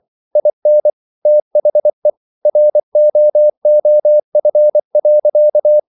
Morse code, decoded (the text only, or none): IN THE ROOF.